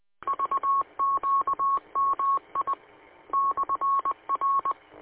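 Morse code, heard as a keyed tone, about 20 wpm, spaced as for 16 wpm, 1100 Hz. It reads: VQM I/R
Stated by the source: Recording from the W8UM 145230 audio archive.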